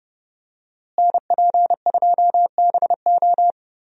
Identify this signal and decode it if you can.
Morse code, heard as a keyed tone, about 31 wpm, 700 Hz. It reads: NP2BO
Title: USA Call Signs - 5 Characters - Repeated 2x - 30wpm